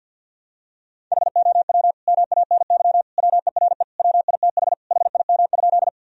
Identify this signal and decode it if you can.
Morse code, heard as a keyed tone, about 50 wpm, 700 Hz. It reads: HOW MANY WERE WITH HIM?
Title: Sentences from Top 1000 Words - Review Part 1 - 50wpm